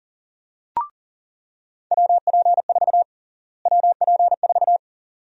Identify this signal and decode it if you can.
Morse code, heard as a keyed tone, about 40 wpm, 700 Hz. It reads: WP4 WP4
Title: USA Call Sign Prefixes - Repeated 2x - 40wpm